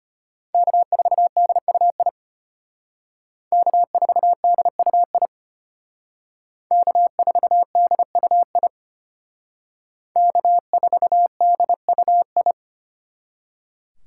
Morse code, heard as a keyed tone, about 32 wpm, 700 Hz. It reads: K4DUI K4DUI K4DUI K4DUI